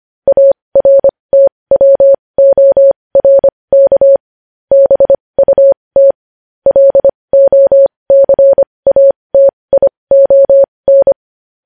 Morse code, heard as a keyed tone, about 25 wpm, spaced as for 20 wpm, 550 Hz. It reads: ARTWORK BUT LOCATION
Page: news-headlines-in-morse-code-at-20-wpm